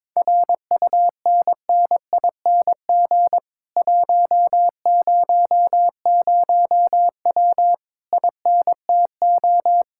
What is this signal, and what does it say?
Morse code, heard as a keyed tone, about 22 wpm, 700 Hz. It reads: RUNNING 100W INTO
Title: Speed-Racing Common QSO Phrases Encore - 22wpm